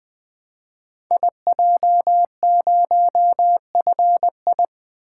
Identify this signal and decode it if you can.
Morse code, heard as a keyed tone, about 20 wpm, 700 Hz. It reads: IJ0FI
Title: Code Groups - 5 Letter:Numbers - 20wpm